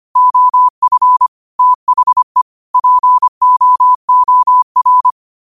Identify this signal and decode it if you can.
Morse code, heard as a keyed tone, about 25 wpm, 1000 Hz. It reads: OF THE POOR